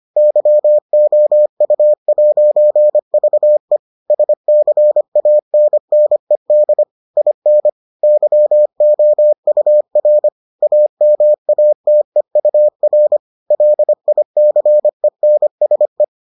Morse code, heard as a keyed tone, about 25 wpm, 600 Hz. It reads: YOU'VE SCANNED IN YOUR AMATEUR LICENSE